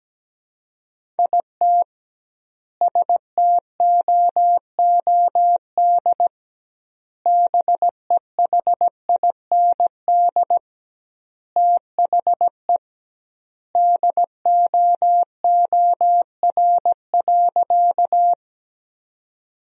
Morse code, heard as a keyed tone, about 17 wpm, 700 Hz. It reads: IT STOOD BEHIND THE DOOR.
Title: 2x Word-Spacing - Sentences from Top 500 Words - 17wpm